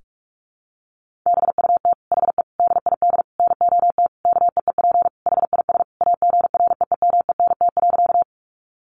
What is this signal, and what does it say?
Morse code, heard as a keyed tone, about 44 wpm, 700 Hz. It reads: BUT HE DID NOT KEEP HIS AGREEMENT.
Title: Sentences from Top 1500 Words 45wpm